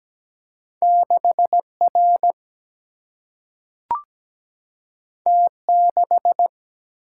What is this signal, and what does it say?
Morse code, heard as a keyed tone, about 17 wpm, 700 Hz. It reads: 6R T6
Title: Single-Letter Single-Number 17wpm